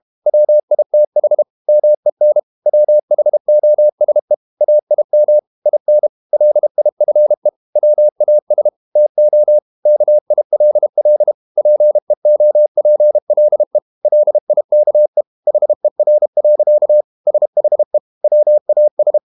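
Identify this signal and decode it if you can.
Morse code, heard as a keyed tone, about 32 wpm, 600 Hz. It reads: WITH MEN WHOSE AIM IN LIFE WAS TO KILL PEOPLE LIKE HER. SHE WAS
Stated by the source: Morse code practice at 32 wpm from CW med Gnister, reading Ken Follett's novel Never in English.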